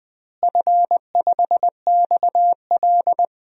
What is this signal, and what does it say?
Morse code, heard as a keyed tone, about 20 wpm, 700 Hz. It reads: F5XL
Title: DXCC-100 Call Signs - 4 Characters - Repeated 3x - 20wpm